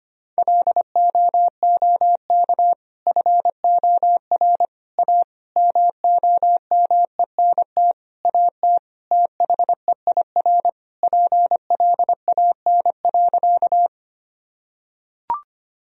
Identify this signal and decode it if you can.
Morse code, heard as a keyed tone, about 25 wpm, 700 Hz. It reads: LOOK FOR A MOMENT AT THEIR PLAN.
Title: Sentences from Top 1000 Words - Review Part 2 - 25wpm